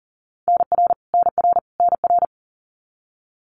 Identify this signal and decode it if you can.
Morse code, heard as a keyed tone, about 40 wpm, 700 Hz. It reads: NR NR NR
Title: Abbreviations Repeated 3x - 40wpm